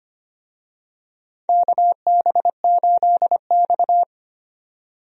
Morse code, heard as a keyed tone, about 25 wpm, 700 Hz. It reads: KB8X
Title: USA Call Signs - 4 Characters - 25wpm